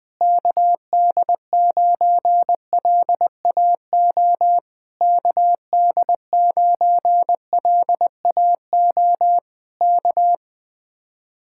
Morse code, heard as a keyed tone, about 21 wpm, 700 Hz. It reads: KD9LAO KD9LAO K
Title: Speed-Racing Common QSO Phrases - 20wpm